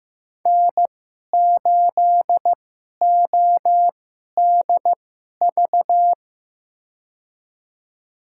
Morse code, heard as a keyed tone, about 15 wpm, spaced as for 11 wpm, 700 Hz. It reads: N8ODV
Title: USA Call Signs - 2x Character Spacing - 15wpm